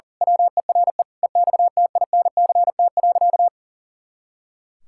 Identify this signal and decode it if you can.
Morse code, heard as a keyed tone, about 40 wpm, 700 Hz. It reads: WERE EXTINCT.
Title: Pangram Challenge v2 - 40wpm